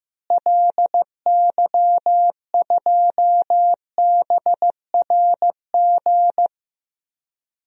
Morse code, heard as a keyed tone, about 15 wpm, 700 Hz. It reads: LY2BRG
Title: DXCC-100 Call Signs - 6 Characters - Repeated 3x - 15wpm